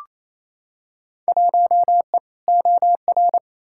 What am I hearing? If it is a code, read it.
Morse code, heard as a keyed tone, about 28 wpm, 700 Hz. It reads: 1E OR
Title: Field Day Exchanges Encore - 28wpm